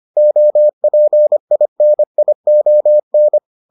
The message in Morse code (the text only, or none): OPINION